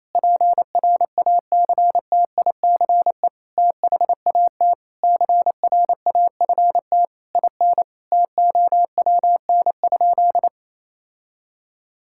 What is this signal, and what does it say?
Morse code, heard as a keyed tone, about 28 wpm, 700 Hz. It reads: PRACTICE THAT CRAFT IN TOWN?
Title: Sentences from Top 1000 Words - Review Part 6 - 28wpm